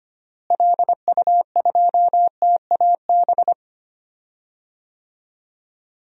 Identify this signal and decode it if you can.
Morse code, heard as a keyed tone, about 25 wpm, 700 Hz. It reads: LU2TAB
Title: DXCC-100 Call Signs - 6 Characters - Repeated 2x - 25wpm